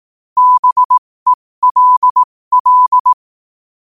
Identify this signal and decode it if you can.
Morse code, heard as a keyed tone, about 18 wpm, spaced as for 15 wpm, 1000 Hz. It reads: BELL
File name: qotdfiles / 15wpm